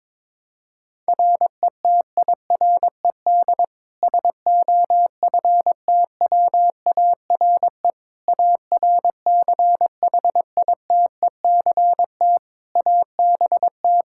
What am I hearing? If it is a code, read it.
Morse code, heard as a keyed tone, about 22 wpm, 700 Hz. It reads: RETIRED SOFTWARE ARCHITECT ABT
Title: Practice QSO - Exchange 3 - More Info - 22wpm